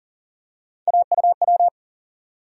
Morse code, heard as a keyed tone, about 40 wpm, 700 Hz. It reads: AUW
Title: USA Call Sign Postfixes - 3 Letters 40wpm